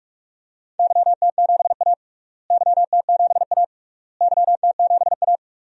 Morse code, heard as a keyed tone, about 46 wpm, 700 Hz. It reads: YT7A YT7A YT7A
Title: DXCC-100 Call Signs - 4 Characters - Repeated 3x - 45wpm